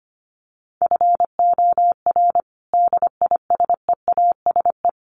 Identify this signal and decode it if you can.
Morse code, heard as a keyed tone, about 25 wpm, 700 Hz. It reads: FOR DISEASE
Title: Sets of 5 Words Encore2 25wpm